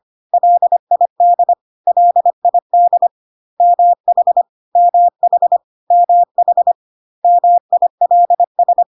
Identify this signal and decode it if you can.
Morse code, heard as a keyed tone, about 25 wpm, 700 Hz. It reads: LID LID MH MH MH MILS
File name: CW abbreviations alphabetical repeated 3 times 25 25wpm